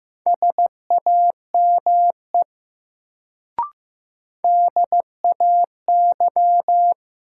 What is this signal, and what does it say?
Morse code, heard as a keyed tone, about 15 wpm, 700 Hz. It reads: SAME DAY
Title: Sentences from Top 200 Words - New Word Drill - 15wpm